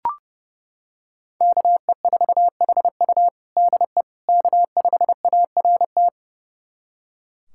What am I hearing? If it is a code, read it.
Morse code, heard as a keyed tone, about 30 wpm, 700 Hz. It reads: KE4HU DE K5ART
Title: USA Callsign DE Callsign - 5 Characters - 30wpm